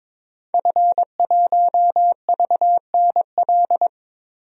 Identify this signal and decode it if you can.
Morse code, heard as a keyed tone, about 22 wpm, 700 Hz. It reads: F1VNL
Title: DXCC-100 Call Signs - 5 Characters - 22wpm